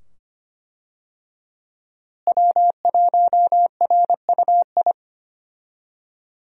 Morse code, heard as a keyed tone, about 25 wpm, 700 Hz. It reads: W1RUI
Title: USA Call Signs - 5 Characters - Repeated 2x - 25wpm